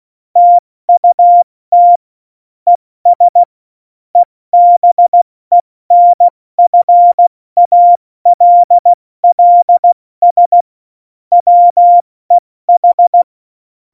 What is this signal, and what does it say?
Morse code, heard as a keyed tone, about 16 wpm, 700 Hz. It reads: TUT ES EBENFALLS WEH
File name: Morsekurs / Witze / Witz 09. Januar 2018, 70 F80